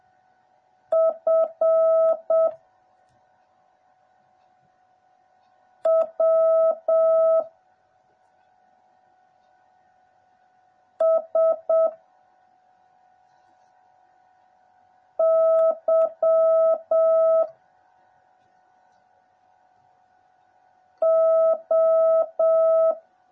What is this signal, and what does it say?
Morse code, heard as a keyed tone, about 7 wpm, 650 Hz. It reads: F W S Y O
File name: jh8cbh / 2021cw / cw09a